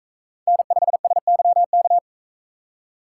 Morse code, heard as a keyed tone, about 42 wpm, 700 Hz. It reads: N5SYK